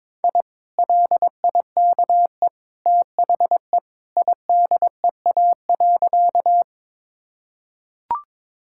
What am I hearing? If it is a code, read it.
Morse code, heard as a keyed tone, about 21 wpm, 700 Hz. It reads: I LIKE THE IDEA.
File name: practice / 22wpm / Sentences with 2-Word Phrases - 22wpm